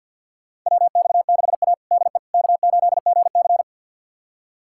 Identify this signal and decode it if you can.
Morse code, heard as a keyed tone, about 49 wpm, 700 Hz. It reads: WX6A DE K8GC
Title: USA Callsign DE Callsign - 4 Characters - 50wpm